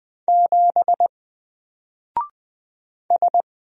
Morse code, heard as a keyed tone, about 20 wpm, 700 Hz. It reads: 7 S
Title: Single Character (Letter:Number:Punctuation:Prosign) Alt 20wpm